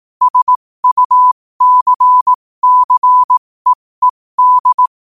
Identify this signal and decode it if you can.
Morse code, heard as a keyed tone, about 18 wpm, spaced as for 15 wpm, 1000 Hz. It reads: SUCCEED
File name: qotdfiles / 15wpm